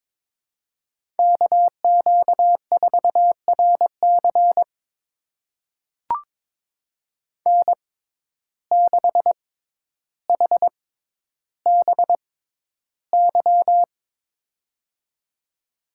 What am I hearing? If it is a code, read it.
Morse code, heard as a keyed tone, about 22 wpm, spaced as for 11 wpm, 700 Hz. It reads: KQ4RC N6HBY
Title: USA Call Signs - 5 Characters - 6x Character Spacing - 22wpm